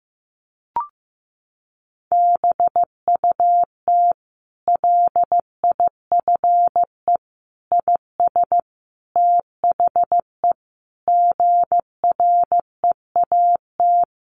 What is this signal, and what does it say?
Morse code, heard as a keyed tone, about 15 wpm, 700 Hz. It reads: BUT LIFE IS THE GREAT
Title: Sentences from Top 900 Words 15wpm